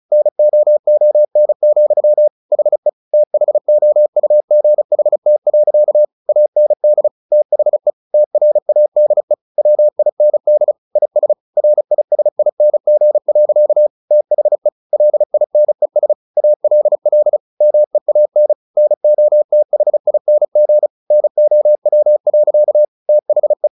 Morse code, heard as a keyed tone, about 34 wpm, 600 Hz. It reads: NOON, HE THOUGHT. AND THE TRADE WIND IS RISING. THE LINES ALL MEAN NOTHING NOW. THE
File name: the old man and the sea 2 35wpm